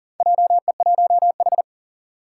Morse code, heard as a keyed tone, about 40 wpm, 700 Hz. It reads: JE1H
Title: Four Challenging Letter-Number Combinations 40wpm